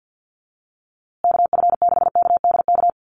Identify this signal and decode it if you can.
Morse code, heard as a keyed tone, about 49 wpm, 700 Hz. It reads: KF6KDK
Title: USA Call Signs Repeated 3x 50wpm